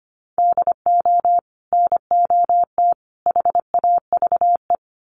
Morse code, heard as a keyed tone, about 25 wpm, 700 Hz. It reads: DO NOT HAVE
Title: Sets of 5 Words 25wpm